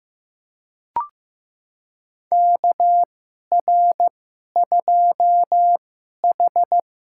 Morse code, heard as a keyed tone, about 15 wpm, spaced as for 11 wpm, 700 Hz. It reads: KR2H
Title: USA Call Signs - 4 Characters - 2x Character Spacing - 15wpm